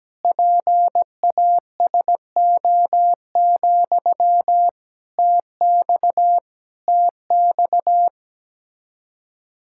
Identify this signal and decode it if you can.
Morse code, heard as a keyed tone, about 17 wpm, 700 Hz. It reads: PASO, TX TX
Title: Practice QSO - Exchange 1 - Beginning - 17wpm